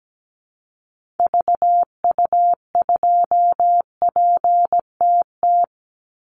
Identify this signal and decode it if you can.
Morse code, heard as a keyed tone, about 17 wpm, 700 Hz. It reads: VU2PTT
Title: Call Signs Repeated 3x - 17wpm